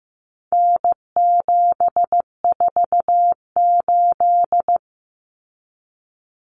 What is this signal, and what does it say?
Morse code, heard as a keyed tone, about 15 wpm, 700 Hz. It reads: N748
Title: Common Cut Numbers - 4-Digit 15wpm